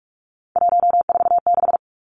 Morse code, heard as a keyed tone, about 44 wpm, 700 Hz. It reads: 146
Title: Numbers - 3-Digit 45wpm